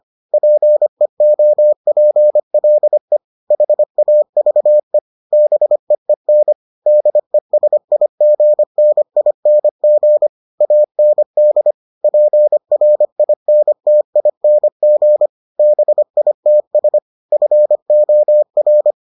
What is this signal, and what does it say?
Morse code, heard as a keyed tone, about 25 wpm, 600 Hz. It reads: PEOPLE HAVE BEEN DESIGNING AND PRINTING BITS FOR